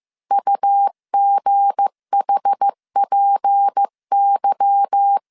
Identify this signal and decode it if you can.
Morse code, heard as a keyed tone, about 15 wpm, 800 Hz. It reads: UGHPY
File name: CW / DET / 1 9 1500C S B 14wpm 3302